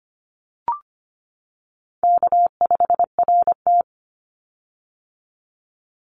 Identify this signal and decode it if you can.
Morse code, heard as a keyed tone, about 25 wpm, 700 Hz. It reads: K5RT
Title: Call Signs - 25wpm